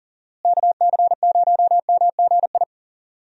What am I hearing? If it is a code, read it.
Morse code, heard as a keyed tone, about 38 wpm, 700 Hz. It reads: KC0MGI
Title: USA Call Signs - 6 Characters - Repeated 3x - 40wpm